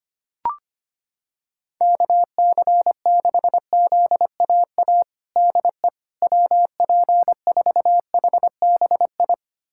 Morse code, heard as a keyed tone, about 25 wpm, 700 Hz. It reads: KC6ZAA DE WP4HBI
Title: USA Callsign DE Callsign - 6 Characters - 25wpm